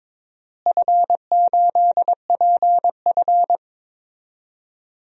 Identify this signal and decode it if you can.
Morse code, heard as a keyed tone, about 21 wpm, 700 Hz. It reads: F8PF
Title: DXCC-100 Call Signs - 4 Characters - Repeated 2x - 22wpm